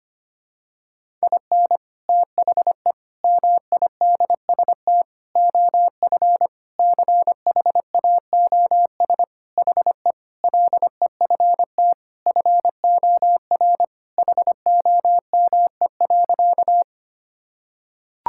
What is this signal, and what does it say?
Morse code, heard as a keyed tone, about 25 wpm, 700 Hz. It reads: IN THE MIDST OF CHAOS HE LEFT FOR HOME.
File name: practice / 25wpm / Sentences with 4-Word Phrases - 25wpm